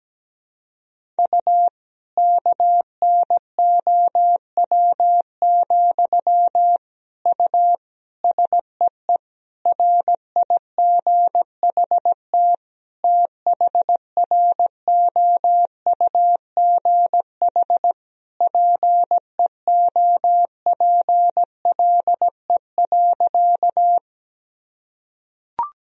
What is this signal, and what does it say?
Morse code, heard as a keyed tone, about 17 wpm, 700 Hz. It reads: U KNOW, U SEE RIGHT THROUGH PEOPLE.
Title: Sentences from Top 200 Words and Abbreviations 17wpm